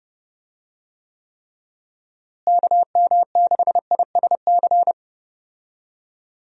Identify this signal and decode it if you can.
Morse code, heard as a keyed tone, about 30 wpm, 700 Hz. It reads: KM6ISC
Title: USA Call Signs - 6 Characters - Repeated 3x - 30wpm